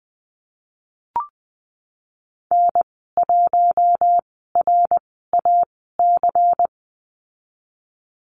Morse code, heard as a keyed tone, about 20 wpm, spaced as for 15 wpm, 700 Hz. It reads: N1RAC
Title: 122 Lesson 20g - Number 3 - Call Signs 20wpm